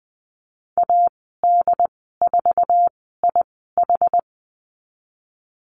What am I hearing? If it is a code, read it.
Morse code, heard as a keyed tone, about 19 wpm, spaced as for 14 wpm, 700 Hz. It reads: AD4IH
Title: 076 Lesson 13g - Letter L - Call Signs 20wpm